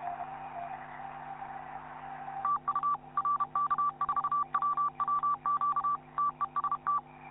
Morse code, heard as a keyed tone, about 31 wpm, 1160 Hz, with noise. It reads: TU RK4WWQ TEST
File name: QSO clips / YI9PSE-20m- -0519Z